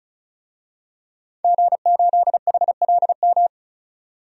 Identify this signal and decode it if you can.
Morse code, heard as a keyed tone, about 34 wpm, 700 Hz. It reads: G8HLM
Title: DXCC-100 Call Signs - 5 Characters - 35wpm